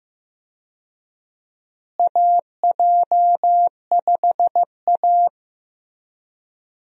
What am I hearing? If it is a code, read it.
Morse code, heard as a keyed tone, about 15 wpm, 700 Hz. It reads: AJ5A
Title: USA Call Signs - 4 Characters - Repeated 2x - 15wpm